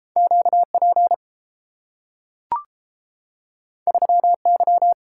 Morse code, heard as a keyed tone, about 30 wpm, 700 Hz. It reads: QP 3Y